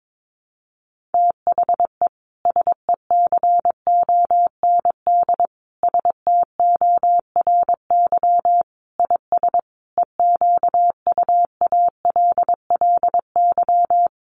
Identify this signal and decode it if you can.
Morse code, heard as a keyed tone, about 22 wpm, 700 Hz. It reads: THE SECOND STORY IS EQUALLY
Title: Sentences from Top 1400 Words 22wpm